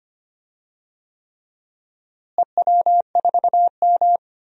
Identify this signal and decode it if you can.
Morse code, heard as a keyed tone, about 25 wpm, 700 Hz. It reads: EW4M
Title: DXCC-100 Call Signs - 4 Characters - Repeated 3x - 25wpm